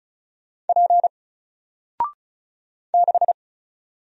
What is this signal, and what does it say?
Morse code, heard as a keyed tone, about 36 wpm, 700 Hz. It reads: P 6
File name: Single-Letter-Number-35wpm